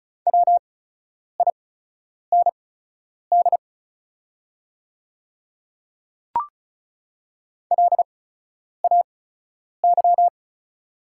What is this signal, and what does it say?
Morse code, heard as a keyed tone, about 36 wpm, spaced as for 10 wpm, 700 Hz. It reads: WIND LAY
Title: 8x Character Spacing - Top 500 Words - 35wpm